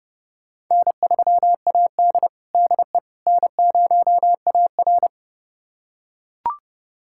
Morse code, heard as a keyed tone, about 32 wpm, 700 Hz. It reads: N3AD DE N0AR
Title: USA Callsign DE Callsign - 4 Characters - 30wpm